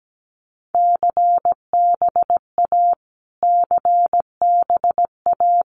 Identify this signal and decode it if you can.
Morse code, heard as a keyed tone, about 17 wpm, 700 Hz. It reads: CBA CBA
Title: Abbreviations Repeated 3x - 17wpm